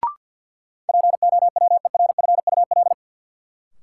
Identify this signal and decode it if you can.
Morse code, heard as a keyed tone, about 53 wpm, 700 Hz. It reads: POWERFUL
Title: Sets of 5 Words Encore - New Word Drill - 50wpm